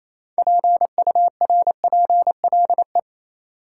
Morse code, heard as a keyed tone, about 28 wpm, 700 Hz. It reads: PURPLE
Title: Six Letter Words - 28wpm